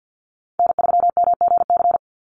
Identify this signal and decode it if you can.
Morse code, heard as a keyed tone, about 50 wpm, 700 Hz. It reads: N3MGC
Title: USA Call Signs 50wpm